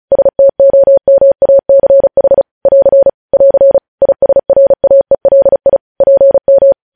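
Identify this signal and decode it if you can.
Morse code, heard as a keyed tone, about 36 wpm, 550 Hz. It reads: STOMACH <AR> <AR> ISRAELI PM
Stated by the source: News headlines sent in Morse at 35 wpm.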